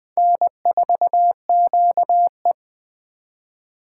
Morse code, heard as a keyed tone, about 20 wpm, 700 Hz. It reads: N4QE
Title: Speed Racing - USA 1x2 Call Signs 17wpm